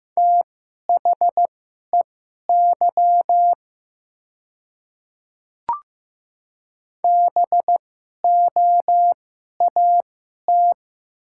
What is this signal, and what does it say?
Morse code, heard as a keyed tone, about 15 wpm, spaced as for 11 wpm, 700 Hz. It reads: THEY BOAT
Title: 2x Character Spacing - Top 500 Words - 15wpm